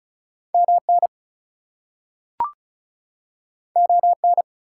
Morse code, heard as a keyed tone, about 36 wpm, 700 Hz. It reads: MN ON
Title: US States and Canadian Provinces 35wpm